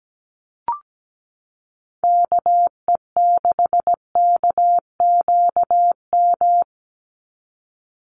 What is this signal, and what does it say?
Morse code, heard as a keyed tone, about 17 wpm, 700 Hz. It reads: KE6KQM
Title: USA Call Signs 17wpm